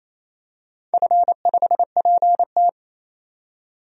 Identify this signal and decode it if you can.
Morse code, heard as a keyed tone, about 28 wpm, 700 Hz. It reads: F5PT
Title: DXCC-100 Call Signs - 4 Characters - 28wpm